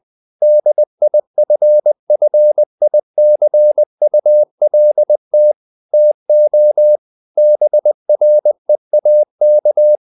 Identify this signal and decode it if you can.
Morse code, heard as a keyed tone, about 20 wpm, 600 Hz. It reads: DIFFICULT TO BREAK
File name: the old man and the sea / the old man and the sea part 1 20wpm